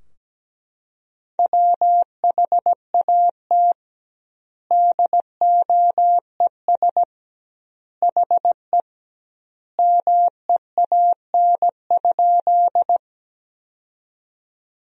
Morse code, heard as a keyed tone, about 17 wpm, 700 Hz. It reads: WHAT DOES HE MEAN?
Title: 2x Word-Spacing - Sentences from Top 200 Words - 17wpm